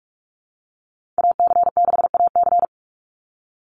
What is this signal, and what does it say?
Morse code, heard as a keyed tone, about 44 wpm, 700 Hz. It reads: AC6AC
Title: Call Signs Repeated 3x - 45wpm